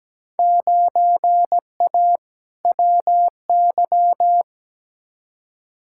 Morse code, heard as a keyed tone, about 17 wpm, 700 Hz. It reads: 9A WY
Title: Field Day Exchanges Encore - 17wpm